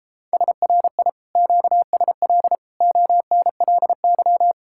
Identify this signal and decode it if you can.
Morse code, heard as a keyed tone, about 31 wpm, 700 Hz. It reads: SRI QSL ONLY